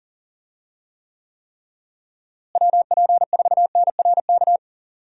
Morse code, heard as a keyed tone, about 40 wpm, 700 Hz. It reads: WP4NRK
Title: USA Call Signs - 6 Characters - Repeated 3x - 40wpm